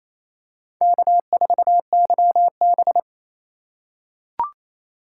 Morse code, heard as a keyed tone, about 28 wpm, 700 Hz. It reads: K4YB
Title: USA Call Signs - 4 Characters - 2x Character Spacing - 28wpm